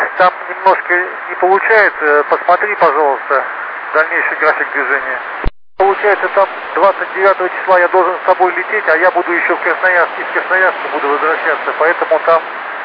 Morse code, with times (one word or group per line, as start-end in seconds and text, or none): none